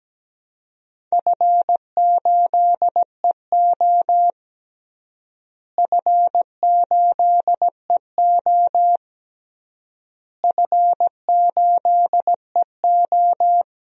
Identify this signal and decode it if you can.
Morse code, heard as a keyed tone, about 17 wpm, 700 Hz. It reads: F8EO F8EO F8EO
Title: DXCC-100 Call Signs - 4 Characters - Repeated 3x - 17wpm